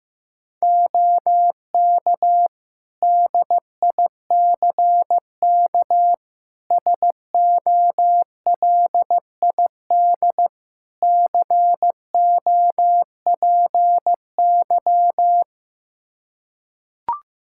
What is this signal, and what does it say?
Morse code, heard as a keyed tone, about 15 wpm, 700 Hz. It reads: OK DICK SOLID COPY
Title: Practice QSO - Exchange 3 - More Info - 15wpm